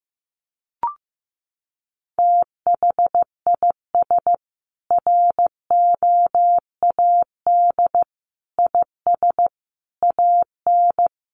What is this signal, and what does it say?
Morse code, heard as a keyed tone, about 15 wpm, 700 Hz. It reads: THIS ROAD IS AN
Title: Sentences from Top 1400 Words 15wpm